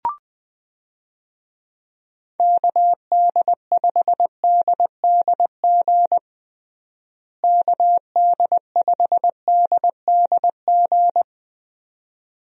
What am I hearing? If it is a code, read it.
Morse code, heard as a keyed tone, about 20 wpm, 700 Hz. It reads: KD5DDG KD5DDG
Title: USA Call Signs - 6 Characters - Repeated 2x - 20wpm